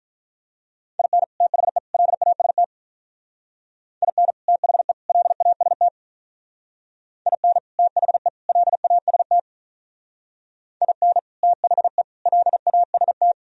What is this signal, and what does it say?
Morse code, heard as a keyed tone, about 43 wpm, 700 Hz. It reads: IN THE LAST IN THE LAST IN THE LAST IN THE LAST
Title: Speed-Racing - Sets of 3 Words 35wpm